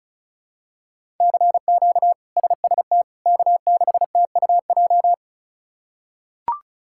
Morse code, heard as a keyed tone, about 36 wpm, 700 Hz. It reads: CQ SST K6TUJ
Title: K1USN SST QSO Practice - 35wpm